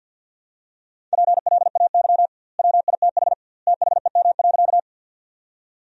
Morse code, heard as a keyed tone, about 50 wpm, 700 Hz. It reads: PLAY WITH THEM.
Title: 2x Word-Spacing - Sentences from Top 200 Words - 50wpm